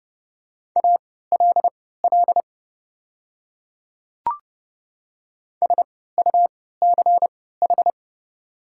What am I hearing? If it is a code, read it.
Morse code, heard as a keyed tone, about 31 wpm, spaced as for 17 wpm, 690 Hz. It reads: ALL SUCH
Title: 3x Character Spacing - Top 200 Words - 30wpm